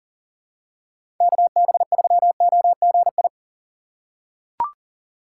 Morse code, heard as a keyed tone, about 37 wpm, 700 Hz. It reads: KB3OGI
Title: USA Call Signs - 6 Characters - 6x Character Spacing - 40wpm